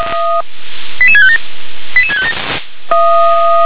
Five-tone selective call (ZVEI), heard as quick runs of five tones, tones of about 70 ms. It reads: 88657 88657